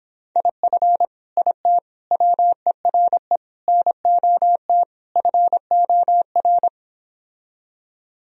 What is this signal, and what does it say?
Morse code, heard as a keyed tone, about 26 wpm, 700 Hz. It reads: IF IT WERE NOT FOR